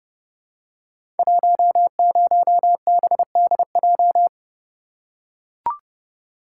Morse code, heard as a keyed tone, about 29 wpm, 700 Hz. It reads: 10BDJ
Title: Code Groups - 5 Letter:Numbers - 30wpm